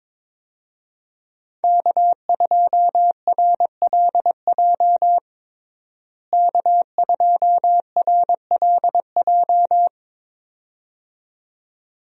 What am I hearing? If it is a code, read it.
Morse code, heard as a keyed tone, about 22 wpm, 700 Hz. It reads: K2RLJ K2RLJ
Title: USA Call Signs - 5 Characters - Repeated 2x - 22wpm